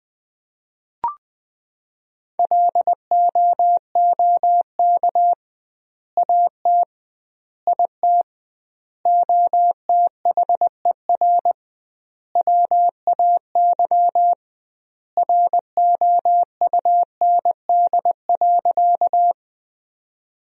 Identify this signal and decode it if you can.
Morse code, heard as a keyed tone, about 21 wpm, 700 Hz. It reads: LOOK AT IT OTHER WAY ROUND.
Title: 2x Word-Spacing - Sentences from Top 200 Words - 20wpm